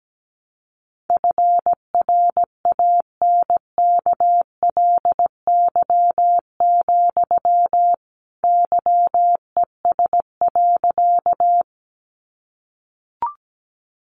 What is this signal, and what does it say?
Morse code, heard as a keyed tone, about 17 wpm, 700 Hz. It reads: FRANKLY, YES.
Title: A Sound Of Thunder 17wpm